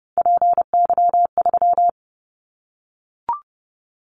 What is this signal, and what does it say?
Morse code, heard as a keyed tone, about 30 wpm, 700 Hz. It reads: PY3
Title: Callsign Prefixes for Top 50 DXCC - 30wpm